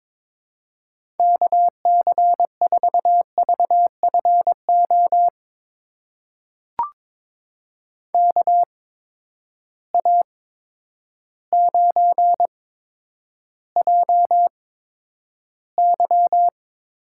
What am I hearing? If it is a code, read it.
Morse code, heard as a keyed tone, about 22 wpm, spaced as for 10 wpm, 700 Hz. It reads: KC4VFO KA9JY
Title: USA Call Signs - 6 Characters - 8x Character Spacing - 22wpm